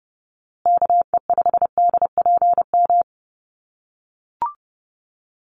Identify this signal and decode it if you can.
Morse code, heard as a keyed tone, about 30 wpm, 700 Hz. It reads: KE5DPM
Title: USA Call Signs Repeated 2x 30wpm